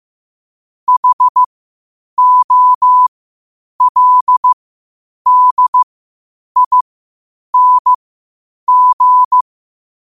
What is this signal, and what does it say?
Morse code, heard as a keyed tone, about 15 wpm, spaced as for 8 wpm, 1000 Hz. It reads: HOLDING
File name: qotdfiles / 8wpm